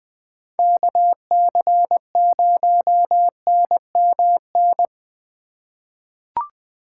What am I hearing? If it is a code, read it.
Morse code, heard as a keyed tone, about 21 wpm, 700 Hz. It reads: KC0NMN
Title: USA Call Signs - 6 Characters - 4x Character Spacing - 20wpm